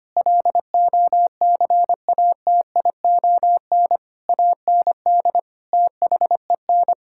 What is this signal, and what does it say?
Morse code, heard as a keyed tone, about 25 wpm, 700 Hz. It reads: LOCATION AND THEN
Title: Sentences from Top 1000 Words - Review Part 6 - 25wpm